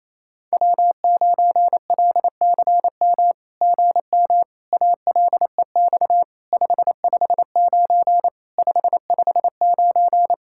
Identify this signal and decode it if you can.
Morse code, heard as a keyed tone, about 27 wpm, 700 Hz. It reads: W9LCM GM ALEX 559 559
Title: SOTA QSO Practice 28wpm